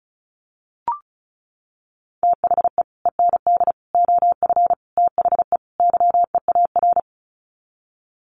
Morse code, heard as a keyed tone, about 34 wpm, 700 Hz. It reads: THE END OF THE YEAR
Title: Sets of 5 Words 35wpm